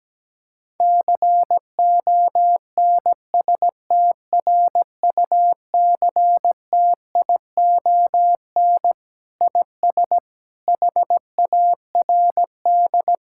Morse code, heard as a keyed tone, about 17 wpm, 700 Hz. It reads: CONSTRUCTION IS HARD
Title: Sentences from Top 1600 Words 17wpm